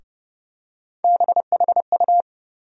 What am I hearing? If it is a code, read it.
Morse code, heard as a keyed tone, about 31 wpm, 700 Hz. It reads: BHU
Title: USA Call Sign Postfixes - 3 Letters 30wpm